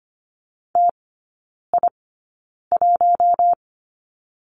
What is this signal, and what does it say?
Morse code, heard as a keyed tone, about 25 wpm, spaced as for 9 wpm, 700 Hz. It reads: TI1
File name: Lesson 17e - Letter u - Three Characters 25wpm